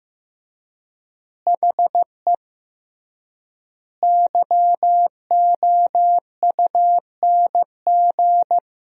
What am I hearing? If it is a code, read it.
Morse code, heard as a keyed tone, about 15 wpm, 700 Hz. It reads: HE YOUNG
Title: 3x Word-Spacing - Sentences from Top 400 Words - 15wpm